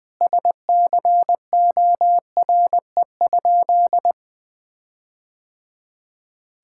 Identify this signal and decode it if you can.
Morse code, heard as a keyed tone, about 21 wpm, 700 Hz. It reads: SCORE?
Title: Sentences from Top 1600 Words 20wpm